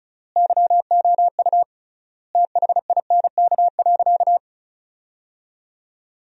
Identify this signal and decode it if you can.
Morse code, heard as a keyed tone, about 34 wpm, 700 Hz. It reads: YOU THINK.
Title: 3x Word-Spacing - Sentences from Top 400 Words - 35wpm